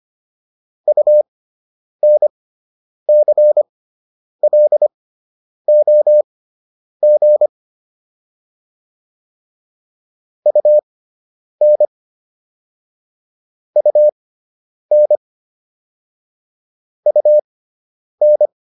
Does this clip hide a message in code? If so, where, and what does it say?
Yes: Morse code from 0.9 to 18.5 s, heard as a keyed tone, about 25 wpm, spaced as for 9 wpm, 600 Hz. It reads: UNCLOG UN UN UN